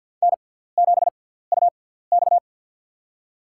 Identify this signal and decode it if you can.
Morse code, heard as a keyed tone, about 48 wpm, spaced as for 17 wpm, 690 Hz. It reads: N7UX
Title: USA Call Signs - 4 Characters - 6x Character Spacing - 50wpm